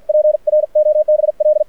Morse code, heard as a keyed tone, about 52 wpm, spaced as for 45 wpm, 600 Hz. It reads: PROBL
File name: archived / CRSnet